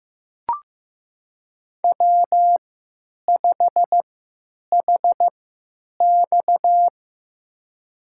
Morse code, heard as a keyed tone, about 15 wpm, spaced as for 9 wpm, 700 Hz. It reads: W5HX
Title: USA Call Signs - 4 Characters - 3x Character Spacing - 15wpm